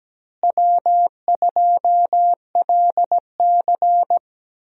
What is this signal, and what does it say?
Morse code, heard as a keyed tone, about 17 wpm, 700 Hz. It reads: W2LC
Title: USA Call Signs - 4 Characters - Repeated 2x - 17wpm